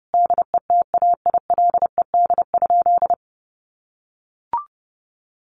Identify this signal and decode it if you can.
Morse code, heard as a keyed tone, about 30 wpm, 700 Hz. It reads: DETAILED?
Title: Sentences from Top 1500 Words 30wpm